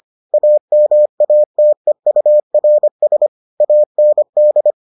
Morse code, heard as a keyed tone, about 25 wpm, 600 Hz. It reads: AMATEURS AND